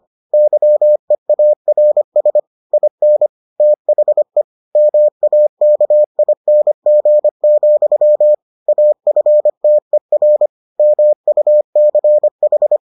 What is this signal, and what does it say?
Morse code, heard as a keyed tone, about 25 wpm, 600 Hz. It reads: YEARS IN THE MAKING, AFTER MUCH